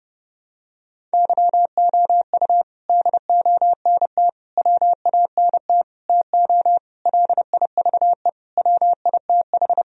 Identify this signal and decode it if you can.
Morse code, heard as a keyed tone, about 30 wpm, 700 Hz. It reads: YOU DONT WANT TO LIVE WITH
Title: Sentences from Top 1000 Words - Review Part 1 - 30wpm